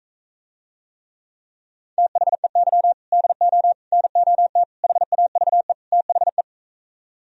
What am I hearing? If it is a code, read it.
Morse code, heard as a keyed tone, about 45 wpm, 700 Hz. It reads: THEY DO NOT HAVE THE